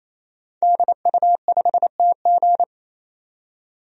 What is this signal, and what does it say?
Morse code, heard as a keyed tone, about 28 wpm, 700 Hz. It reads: DU5TG
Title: DXCC-100 Call Signs - 5 Characters - 28wpm